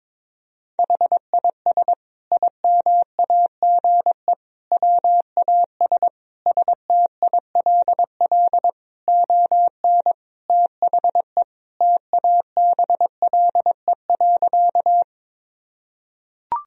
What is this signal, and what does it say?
Morse code, heard as a keyed tone, about 22 wpm, 700 Hz. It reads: HIS IMAGE WAS STILL ON THE TABLE.
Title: Sentences from Top 1000 Words - Review Part 4 - 22wpm